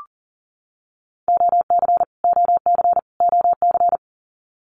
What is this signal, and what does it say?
Morse code, heard as a keyed tone, about 40 wpm, 700 Hz. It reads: OC OC OC
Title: Abbreviations Repeated 3x - 40wpm